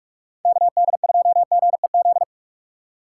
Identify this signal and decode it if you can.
Morse code, heard as a keyed tone, about 49 wpm, 700 Hz. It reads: KD2GEZ